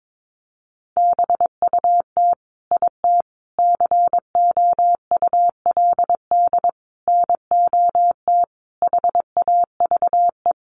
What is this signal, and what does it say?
Morse code, heard as a keyed tone, about 22 wpm, 700 Hz. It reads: BUT IT COULD NOT HAVE
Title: Sentences from Top 1400 Words 22wpm